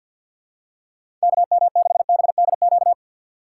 Morse code, heard as a keyed tone, about 53 wpm, 700 Hz. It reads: KM6BDQ
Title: USA Call Signs - 4x Character Spacing - 50wpm